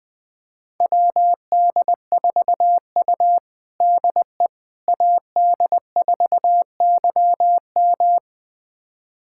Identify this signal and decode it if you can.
Morse code, heard as a keyed tone, about 20 wpm, 700 Hz. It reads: WD4U DE AD4YM
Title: USA Callsign DE Callsign 20wpm